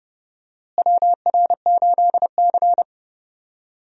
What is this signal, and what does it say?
Morse code, heard as a keyed tone, about 30 wpm, 700 Hz. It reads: WR8C
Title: USA Call Signs - 4 Characters - Repeated 2x - 30wpm